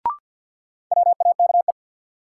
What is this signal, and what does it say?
Morse code, heard as a keyed tone, about 53 wpm, 700 Hz. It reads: WAKE
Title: Sets of 4 Words - New Word Drill - 50wpm